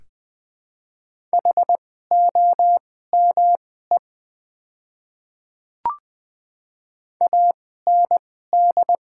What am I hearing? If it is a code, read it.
Morse code, heard as a keyed tone, about 20 wpm, spaced as for 14 wpm, 700 Hz. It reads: HOME AND
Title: 2x Character Spacing - Top 200 Words - 20wpm